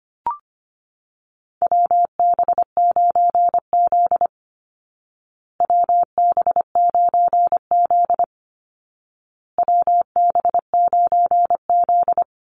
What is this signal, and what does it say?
Morse code, heard as a keyed tone, about 24 wpm, 700 Hz. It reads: WB9Z WB9Z WB9Z
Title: Call Signs Repeated 2x - 25wpm